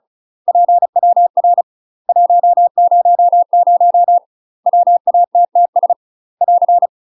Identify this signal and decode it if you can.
Morse code, heard as a keyed tone, about 34 wpm, 700 Hz. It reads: PWR 100 WATTS <AR>
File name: UK MASTER QSOs TRAINING / 35WPM / UK MASTER QSOs 0040